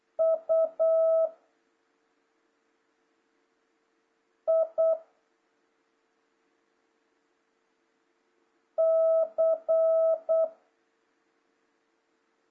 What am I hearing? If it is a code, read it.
Morse code, heard as a keyed tone, about 8 wpm, 650 Hz. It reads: U I C
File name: jh8cbh / 2026cw